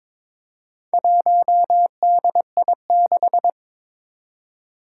Morse code, heard as a keyed tone, about 22 wpm, 700 Hz. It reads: 1DI6
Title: Four Challenging Letter-Number Combinations 22wpm